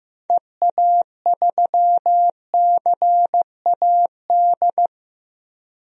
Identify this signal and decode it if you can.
Morse code, heard as a keyed tone, about 15 wpm, 700 Hz. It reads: EA3CAD
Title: DXCC-100 Call Signs - 6 Characters - Repeated 3x - 15wpm